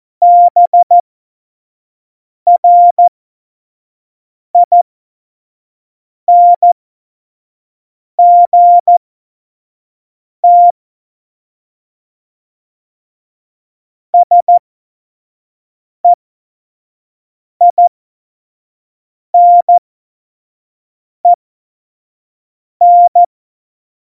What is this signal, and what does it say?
Morse code, heard as a keyed tone, about 14 wpm, spaced as for 5 wpm, 700 Hz. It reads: BRINGT SEINEN
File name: Morsekurs / Witze / Witz 03. November 2020, 25 F70